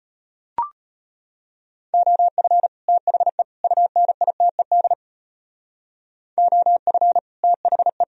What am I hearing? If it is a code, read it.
Morse code, heard as a keyed tone, about 34 wpm, 700 Hz. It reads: OF THE UNITED OF THE